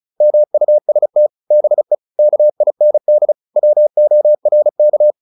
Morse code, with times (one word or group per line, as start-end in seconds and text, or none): 0.2-1.3 MUST
1.5-1.9 BE
2.2-3.3 KIND
3.6-5.1 WORK